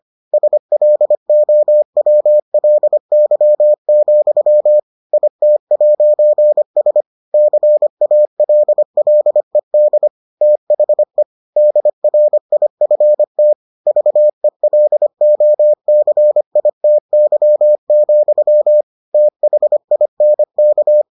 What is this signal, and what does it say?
Morse code, heard as a keyed tone, about 25 wpm, 600 Hz. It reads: SLOWLY, IT'S CALLED THE DRIFT VELOCITY, THINK